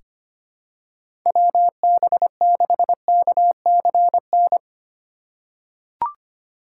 Morse code, heard as a keyed tone, about 25 wpm, 700 Hz. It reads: WB6KCN
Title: USA Call Signs - 6 Characters - 2x Character Spacing - 25wpm